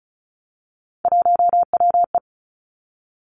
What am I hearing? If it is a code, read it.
Morse code, heard as a keyed tone, about 34 wpm, 700 Hz. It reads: 1WE
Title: Three Challenging Letter-Number Combinations 35wpm